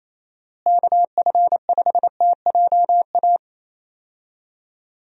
Morse code, heard as a keyed tone, about 27 wpm, 700 Hz. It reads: KF5TJA
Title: USA Call Signs - 6 Characters - Repeated 2x - 28wpm